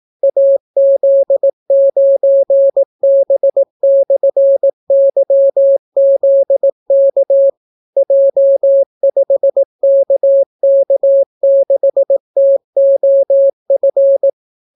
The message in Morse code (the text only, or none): AZ9B/YZK J5KK6TOF